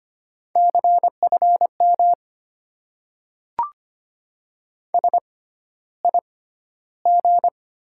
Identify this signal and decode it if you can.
Morse code, heard as a keyed tone, about 25 wpm, spaced as for 13 wpm, 700 Hz. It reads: CFM SIG
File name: Top 90 QSO Elements - 6x Character Spacing - 25wpm